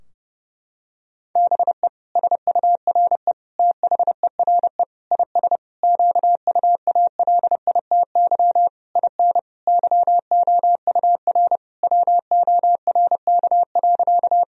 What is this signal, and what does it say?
Morse code, heard as a keyed tone, about 30 wpm, 700 Hz. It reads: BE SURE THERE IS QUALITY IN YOUR WORK.
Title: Sentences from Top 1000 Words - Review Part 4 - 30wpm